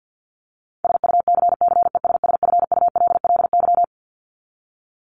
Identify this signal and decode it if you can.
Morse code, heard as a keyed tone, about 50 wpm, 700 Hz. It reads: SUCCESSFULLY
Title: LY Suffix - Top 100 Words with Base Words 50wpm